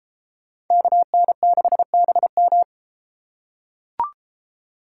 Morse code, heard as a keyed tone, about 32 wpm, 700 Hz. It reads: KN6BM